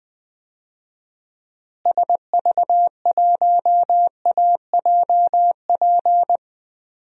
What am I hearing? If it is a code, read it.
Morse code, heard as a keyed tone, about 21 wpm, 700 Hz. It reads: SV1AJP
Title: DXCC-100 Call Signs - 6 Characters - Repeated 3x - 20wpm